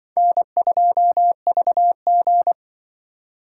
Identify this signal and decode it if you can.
Morse code, heard as a keyed tone, about 25 wpm, 690 Hz. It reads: N2VG
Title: Speed Racing - USA 1x2 Call Signs 20wpm